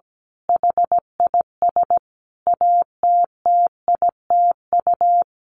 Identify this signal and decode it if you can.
Morse code, heard as a keyed tone, about 17 wpm, 700 Hz. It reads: HIS ATTITU
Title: Sentences from Top 1300 Words 17wpm